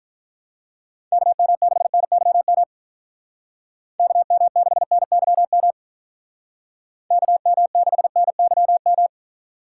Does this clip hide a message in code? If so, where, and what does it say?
Yes: Morse code from 1.1 to 9.1 s, heard as a keyed tone, about 46 wpm, 700 Hz. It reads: KM6NYM KM6NYM KM6NYM